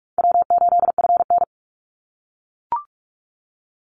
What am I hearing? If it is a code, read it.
Morse code, heard as a keyed tone, about 44 wpm, 700 Hz. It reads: W8FN
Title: Easy Call Signs - 45wpm